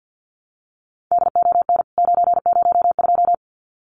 Morse code, heard as a keyed tone, about 50 wpm, 700 Hz. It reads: DON 902
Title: CWT Exchange 50wpm